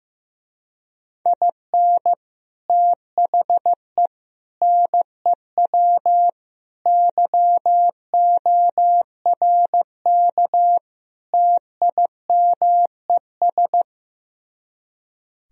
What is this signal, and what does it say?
Morse code, heard as a keyed tone, about 15 wpm, 700 Hz. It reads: IN THE NEW YORK TIMES
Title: Speed-Racing - Sets of 5 Words 15wpm